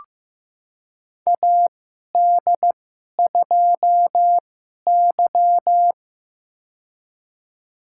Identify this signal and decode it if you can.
Morse code, heard as a keyed tone, about 15 wpm, spaced as for 11 wpm, 700 Hz. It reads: AD2Y
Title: USA Call Signs - 2x Character Spacing - 15wpm